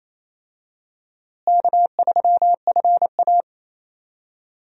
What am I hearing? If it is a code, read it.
Morse code, heard as a keyed tone, about 28 wpm, 700 Hz. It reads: K3FA
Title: USA Call Signs - 4 Characters - Repeated 3x - 28wpm